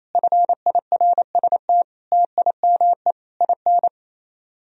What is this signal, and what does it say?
Morse code, heard as a keyed tone, about 33 wpm, 700 Hz. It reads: FIRST TIME IN
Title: Speed-Racing - Sets of 4 Words 28wpm